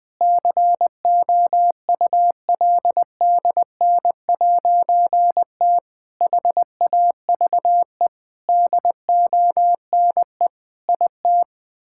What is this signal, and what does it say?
Morse code, heard as a keyed tone, about 20 wpm, 700 Hz. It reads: COULDN'T HAVE DONE IT
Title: Difficult Sentences with Hints - 20wpm